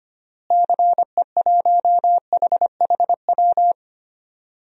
Morse code, heard as a keyed tone, about 25 wpm, 700 Hz. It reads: CE1HHW
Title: DXCC-100 Call Signs - 6 Characters - Repeated 3x - 25wpm